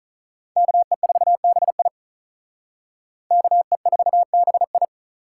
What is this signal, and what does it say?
Morse code, heard as a keyed tone, about 44 wpm, 700 Hz. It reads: KE4BI KE4BI